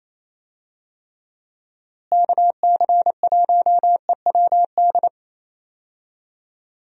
Morse code, heard as a keyed tone, about 28 wpm, 700 Hz. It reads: KC1EWD
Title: USA Call Signs - 6 Characters - Repeated 2x - 28wpm